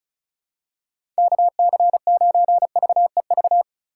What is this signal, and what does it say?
Morse code, heard as a keyed tone, about 36 wpm, 690 Hz. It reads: KC9VEV
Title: USA Call Signs - 6 Characters - 3x Character Spacing - 35wpm